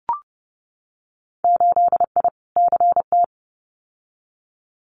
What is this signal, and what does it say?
Morse code, heard as a keyed tone, about 30 wpm, 700 Hz. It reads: 8I CT
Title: Winter Field Day Exchanges Encore 30wpm